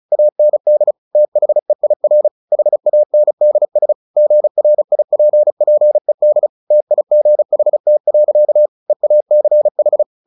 Morse code, heard as a keyed tone, about 34 wpm, 600 Hz. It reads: AND THEIR HANDS GRIPPED TIGHT. EACH
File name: the old man and the sea / the old man and the sea 1 35wpm